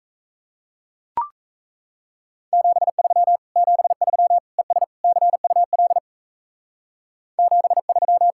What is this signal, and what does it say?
Morse code, heard as a keyed tone, about 34 wpm, 700 Hz. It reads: 73 73 ES CUL 73
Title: Speed-Racing Common QSO Phrases - 28wpm